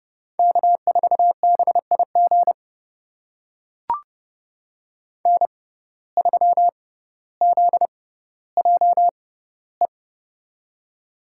Morse code, heard as a keyed tone, about 30 wpm, spaced as for 15 wpm, 700 Hz. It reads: K4BIG N3ZJE
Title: USA Call Signs - 5 Characters - 6x Character Spacing - 30wpm